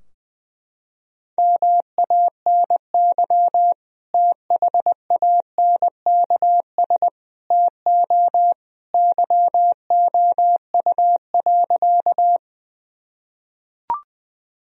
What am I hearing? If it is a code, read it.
Morse code, heard as a keyed tone, about 20 wpm, 700 Hz. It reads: MANY THANKS TO YOU.
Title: Sentences from Top 1000 Words - Review Part 3 - 20wpm